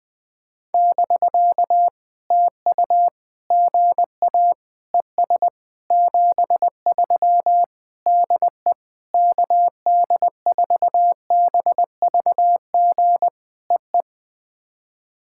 Morse code, heard as a keyed tone, about 20 wpm, 700 Hz. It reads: <BK> TU GA ES 73 DE KD4BVG EE
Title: POTA QSO Practice - 20wpm